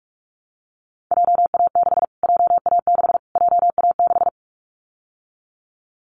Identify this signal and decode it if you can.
Morse code, heard as a keyed tone, about 44 wpm, 700 Hz. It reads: JA6 JA6 JA6
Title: Callsign Prefixes for Top 25 DXCC - Repeated 3x - 45wpm